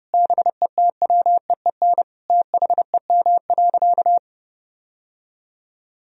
Morse code, heard as a keyed tone, about 30 wpm, 700 Hz. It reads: BETWEEN THEM.
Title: Sentences from Top 1000 Words - Review Part 1 - 30wpm